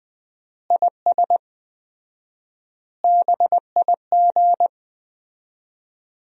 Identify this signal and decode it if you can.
Morse code, heard as a keyed tone, about 20 wpm, 700 Hz. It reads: IS BIG
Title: 4x Word-Spacing - Sentences from Top 400 Words - 20wpm